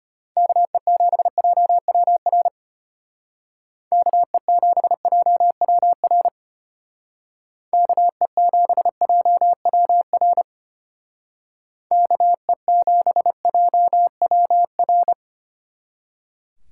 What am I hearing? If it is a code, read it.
Morse code, heard as a keyed tone, about 32 wpm, 700 Hz. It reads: KE7JWR KE7JWR KE7JWR KE7JWR